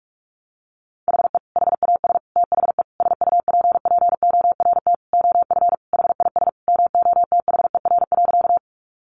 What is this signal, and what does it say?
Morse code, heard as a keyed tone, about 44 wpm, 700 Hz. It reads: HE HAS THE SUPPORT OF HIS MOTHER.
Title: Sentences from Top 800 Words 45wpm